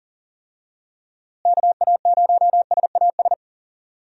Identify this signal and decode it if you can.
Morse code, heard as a keyed tone, about 40 wpm, 700 Hz. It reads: KA0SAS
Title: USA Call Signs - 6 Characters - Repeated 3x - 40wpm